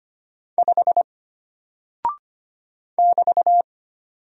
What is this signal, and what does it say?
Morse code, heard as a keyed tone, about 25 wpm, 700 Hz. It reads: 5 <BT>
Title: Single Character (Letter:Number:Punctuation:Prosign) Alt 25wpm